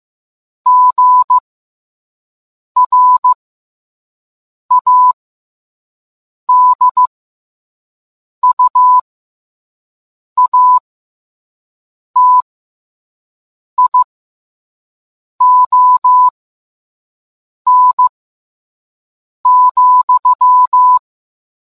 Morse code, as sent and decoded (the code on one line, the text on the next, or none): --. .-. .- -.. ..- .- - .. --- -. --..--
GRADUATION,